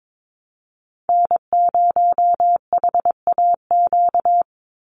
Morse code, heard as a keyed tone, about 22 wpm, 700 Hz. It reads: N0HAQ
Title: USA Call Signs 22wpm